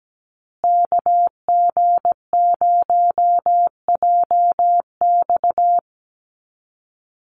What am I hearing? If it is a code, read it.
Morse code, heard as a keyed tone, about 17 wpm, 700 Hz. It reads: KG0JX
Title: USA Call Signs Repeated 3x 17wpm